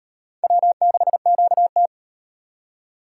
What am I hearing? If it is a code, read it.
Morse code, heard as a keyed tone, about 35 wpm, 700 Hz. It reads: W6QT